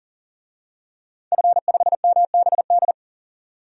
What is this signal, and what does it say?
Morse code, heard as a keyed tone, about 38 wpm, 700 Hz. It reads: F5MBD
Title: DXCC-100 Call Signs - 5 Characters - 40wpm